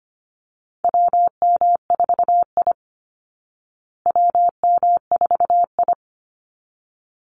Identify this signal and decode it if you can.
Morse code, heard as a keyed tone, about 25 wpm, 700 Hz. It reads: WM4I WM4I
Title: Call Signs Repeated 3x - 25wpm